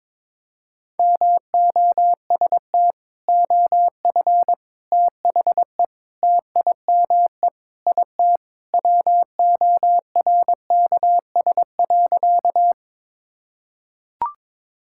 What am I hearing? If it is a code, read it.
Morse code, heard as a keyed tone, about 22 wpm, 700 Hz. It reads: MOST OF THE TIME IT WORKS.
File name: practice / 22wpm / Sentences with 3-Word Phrases - 22wpm